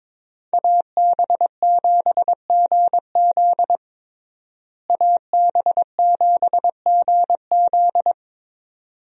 Morse code, heard as a keyed tone, about 22 wpm, 700 Hz. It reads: AB7GZ AB7GZ
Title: USA Call Signs - 5 Characters - Repeated 2x - 22wpm